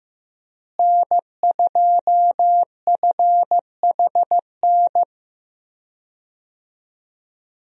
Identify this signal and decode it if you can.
Morse code, heard as a keyed tone, about 15 wpm, 700 Hz. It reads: N2FHN
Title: USA Call Signs - 5 Characters - Repeated 2x - 15wpm